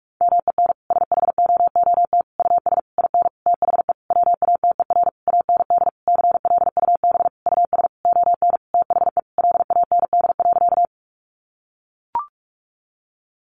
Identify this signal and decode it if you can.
Morse code, heard as a keyed tone, about 44 wpm, 700 Hz. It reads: MEN SHOOT US IN THE WATER AND CLUB US ON THE LAND.
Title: Sentences from Top 1200 Words 45wpm